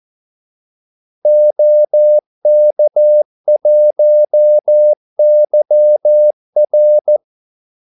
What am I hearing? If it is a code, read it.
Morse code, heard as a keyed tone, about 14 wpm, 600 Hz. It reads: OK1YR
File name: Callsigns-14wpm